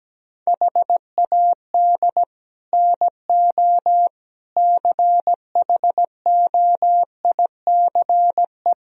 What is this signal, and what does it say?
Morse code, heard as a keyed tone, about 17 wpm, 700 Hz. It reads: HAD NO CHOICE
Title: Speed-Racing - Sets of 5 Words 17wpm